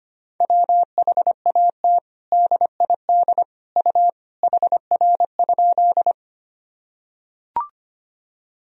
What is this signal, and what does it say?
Morse code, heard as a keyed tone, about 25 wpm, 700 Hz. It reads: WHAT DID U HR?
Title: Sentences from Top 300 Words and Abbreviations 25wpm